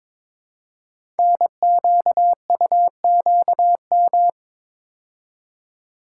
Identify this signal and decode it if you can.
Morse code, heard as a keyed tone, about 22 wpm, 700 Hz. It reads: NQUQM
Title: Code Groups - 5 Letters - 22wpm